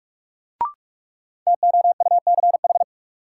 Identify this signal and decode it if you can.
Morse code, heard as a keyed tone, about 44 wpm, 700 Hz. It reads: TOUCH
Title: Sentences from Top 1200 Words - New Word Drill Alt - 45wpm